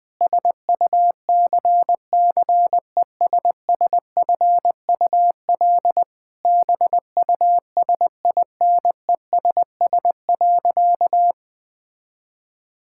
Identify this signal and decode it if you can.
Morse code, heard as a keyed tone, about 21 wpm, 700 Hz. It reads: SUCCESSFUL BUSINESS.
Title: Sentences from Top 1000 Words - Review Part 6 - 20wpm